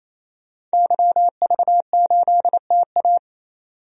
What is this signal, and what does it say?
Morse code, heard as a keyed tone, about 28 wpm, 700 Hz. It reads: YV8TA
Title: DXCC-100 Call Signs - 5 Characters - 28wpm